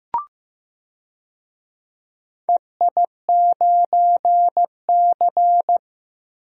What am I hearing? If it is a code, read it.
Morse code, heard as a keyed tone, about 15 wpm, 700 Hz. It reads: EI9C
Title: DXCC-100 Call Signs - 4 Characters - 15wpm